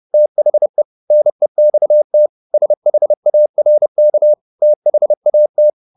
Morse code, heard as a keyed tone, about 30 wpm, 600 Hz. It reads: THE NEXT SHARK THAT